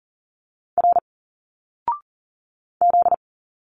Morse code, heard as a keyed tone, about 40 wpm, 700 Hz. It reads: R Z
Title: Single Letters 40wpm